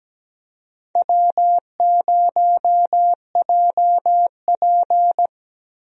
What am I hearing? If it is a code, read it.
Morse code, heard as a keyed tone, about 17 wpm, 700 Hz. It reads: W0JP
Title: USA Call Signs - 4 Characters - Repeated 2x - 17wpm